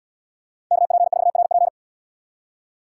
Morse code, heard as a keyed tone, about 70 wpm, 690 Hz. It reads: KP4RP